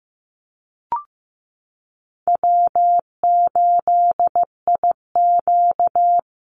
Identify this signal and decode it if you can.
Morse code, heard as a keyed tone, about 15 wpm, 700 Hz. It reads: W8IQ
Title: Call Signs Repeated 3x - 15wpm